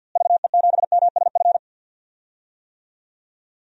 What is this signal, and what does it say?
Morse code, heard as a keyed tone, about 53 wpm, 700 Hz. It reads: VE7MSF
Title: DXCC-100 Call Signs - 6 Characters - Repeated 2x - 50wpm